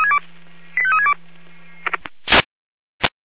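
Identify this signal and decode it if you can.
Five-tone selective call (ZVEI), heard as quick runs of five tones, tones of about 70 ms. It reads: 85362
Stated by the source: ZVEI five-tone call received on SDR.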